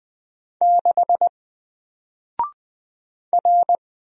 Morse code, heard as a keyed tone, about 20 wpm, 700 Hz. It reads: 6 R
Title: Single Letter-Number Alt 20wpm